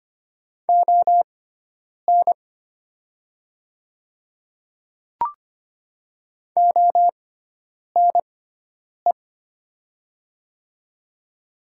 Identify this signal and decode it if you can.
Morse code, heard as a keyed tone, about 25 wpm, spaced as for 9 wpm, 700 Hz. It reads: ON ONE
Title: 6x Character Spacing - Top 100 Words - 25wpm